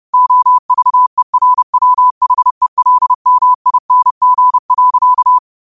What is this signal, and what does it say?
Morse code, heard as a keyed tone, about 30 wpm, 1000 Hz. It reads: OVERWHELMING.
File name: qotdfiles / 30wpm